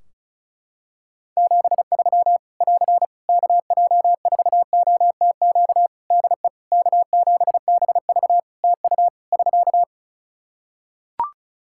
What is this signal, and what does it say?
Morse code, heard as a keyed tone, about 34 wpm, 700 Hz. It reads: 73 <AR> KJ4OTQ DE K7BV TU <SK>
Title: K1USN SST QSO Practice 35wpm